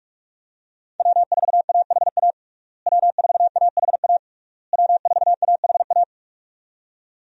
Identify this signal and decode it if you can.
Morse code, heard as a keyed tone, about 46 wpm, 700 Hz. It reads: W4AHA W4AHA W4AHA
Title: USA Call Signs - 5 Characters - Repeated 3x - 45wpm